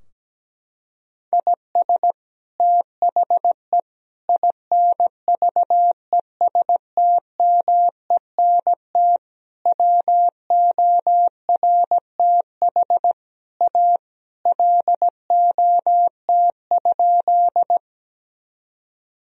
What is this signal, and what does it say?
Morse code, heard as a keyed tone, about 17 wpm, 700 Hz. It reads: IS THE INVESTMENT WORTH A LOT?
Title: Sentences from Top 1000 Words - Review Part 4 - 17wpm